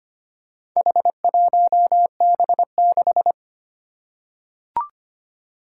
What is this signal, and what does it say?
Morse code, heard as a keyed tone, about 25 wpm, 700 Hz. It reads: H1B6
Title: Four Challenging Letter-Number Combinations 25wpm